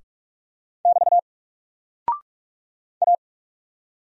Morse code, heard as a keyed tone, about 44 wpm, 700 Hz. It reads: <BT> A
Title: Single Character (Letter:Number:Punctuation:Prosign) 45wpm